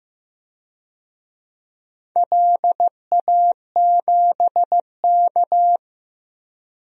Morse code, heard as a keyed tone, about 16 wpm, 700 Hz. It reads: LA7K
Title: DXCC-100 Call Signs - 4 Characters - Repeated 2x - 15wpm